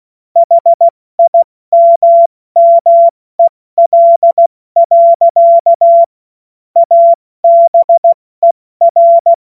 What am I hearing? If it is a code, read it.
Morse code, heard as a keyed tone, about 16 wpm, 680 Hz. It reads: HIMMEL. ABER
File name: Morsekurs / Witze / Witz 30. August 2022 70 F80